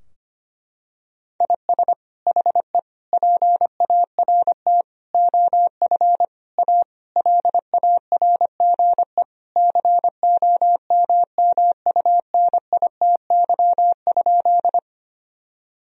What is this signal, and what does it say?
Morse code, heard as a keyed tone, about 25 wpm, 700 Hz. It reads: IS HE PART OF A LARGE COMMUNITY?
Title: Sentences from Top 1000 Words - Review Part 4 - 25wpm